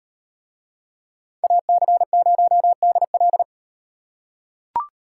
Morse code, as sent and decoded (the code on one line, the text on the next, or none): .- -.-. ----- -.. .-..
AC0DL